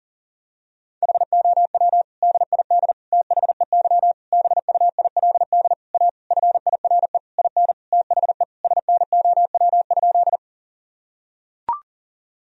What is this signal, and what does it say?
Morse code, heard as a keyed tone, about 40 wpm, 700 Hz. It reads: HOW DID THEY BUILD A FIRE IN THE SNOW?
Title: Sentences from Top 1000 Words - Review Part 3 - 40wpm